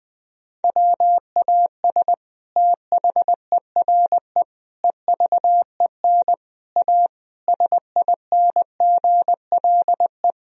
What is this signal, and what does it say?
Morse code, heard as a keyed tone, about 20 wpm, 700 Hz. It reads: WAS THERE EVEN A SINGLE
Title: Sentences from Top 1000 Words - Review Part 4 - 20wpm